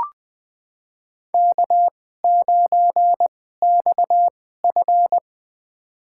Morse code, heard as a keyed tone, about 20 wpm, spaced as for 14 wpm, 700 Hz. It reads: K9XF
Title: USA Call Signs - 4 Characters - 2x Character Spacing - 20wpm